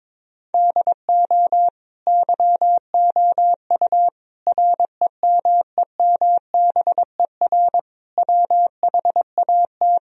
Morse code, heard as a keyed tone, about 21 wpm, 700 Hz. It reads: DO YOU REMEMBER WHAT
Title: Sentences from Top 1000 Words - Review Part 4 - 22wpm